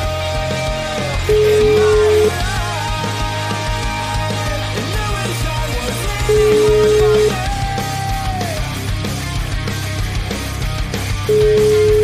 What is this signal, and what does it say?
Ringback tone (Europe, 425 Hz), with noise.